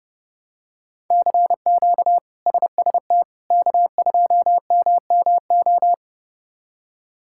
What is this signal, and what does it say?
Morse code, heard as a keyed tone, about 32 wpm, 700 Hz. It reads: CQ SST K2MMO
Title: K1USN SST QSO Practice - 30wpm